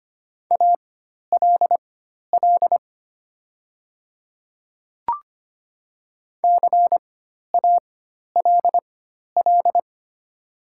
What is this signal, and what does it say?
Morse code, heard as a keyed tone, about 25 wpm, spaced as for 12 wpm, 690 Hz. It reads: ALL CALL
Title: 4x Character Spacing - Top 100 Words - 25wpm